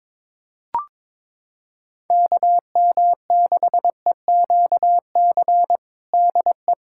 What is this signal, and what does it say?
Morse code, heard as a keyed tone, about 22 wpm, 700 Hz. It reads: KM6EQC DE
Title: USA Callsign DE Callsign 22wpm